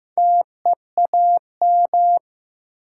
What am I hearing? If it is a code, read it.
Morse code, heard as a keyed tone, about 15 wpm, 700 Hz. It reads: TEAM
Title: Sentences from Top 1500 Words - New Word Drill Alt - 15wpm